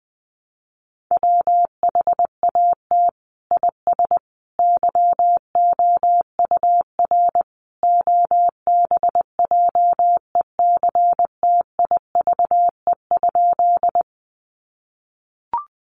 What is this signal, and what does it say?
Morse code, heard as a keyed tone, about 20 wpm, 700 Hz. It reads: WHAT IS YOUR OBJECTIVE?
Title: Sentences from Top 1200 Words 20wpm